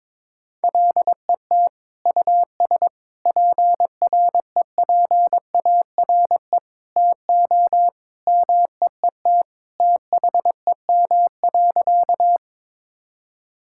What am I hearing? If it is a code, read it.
Morse code, heard as a keyed tone, about 22 wpm, 700 Hz. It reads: LET US PREPARE TO MEET THEM.
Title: Sentences from Top 1000 Words - Review Part 6 - 22wpm